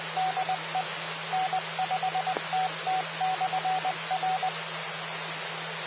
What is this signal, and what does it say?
Morse code, heard as a keyed tone, about 20 wpm, 750 Hz. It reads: DE N5TT/R
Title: Austin, TX Amateur Repeaters